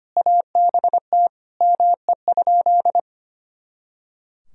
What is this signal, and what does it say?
Morse code, heard as a keyed tone, about 25 wpm, 700 Hz. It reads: ABT ME?
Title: Sentences from Top 200 Words and Abbreviations 25wpm